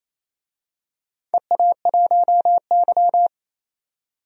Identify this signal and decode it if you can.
Morse code, heard as a keyed tone, about 27 wpm, 700 Hz. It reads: EA1Y
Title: DXCC-100 Call Signs - 4 Characters - 28wpm